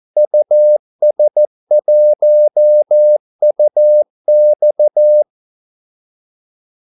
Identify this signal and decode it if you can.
Morse code, heard as a keyed tone, about 14 wpm, 600 Hz. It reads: US1UX
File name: Callsigns-14wpm